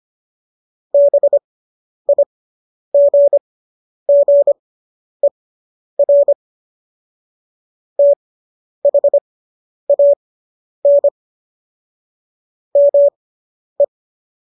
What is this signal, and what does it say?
Morse code, heard as a keyed tone, about 25 wpm, spaced as for 10 wpm, 570 Hz. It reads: BIGGER THAN ME